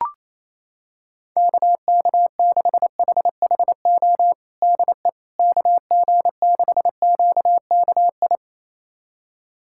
Morse code, heard as a keyed tone, about 28 wpm, 700 Hz. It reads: KK6HHO DE KG6QKI
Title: USA Callsign DE Callsign 28wpm